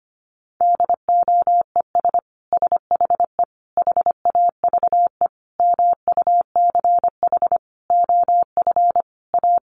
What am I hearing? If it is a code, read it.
Morse code, heard as a keyed tone, about 25 wpm, 700 Hz. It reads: DOES SHE HAVE MUCH OF A
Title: Sentences from Top 1400 Words 25wpm